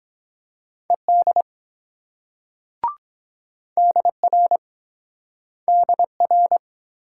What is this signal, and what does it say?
Morse code, heard as a keyed tone, about 23 wpm, 690 Hz. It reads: ED DR DR